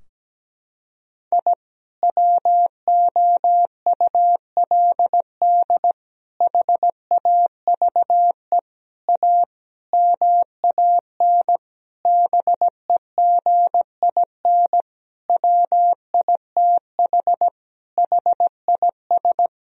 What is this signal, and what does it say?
Morse code, heard as a keyed tone, about 17 wpm, 700 Hz. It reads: I WOULD HAVE A MAN BEGIN WITH HIS
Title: Sentences from Top 1000 Words - Review Part 2 - 17wpm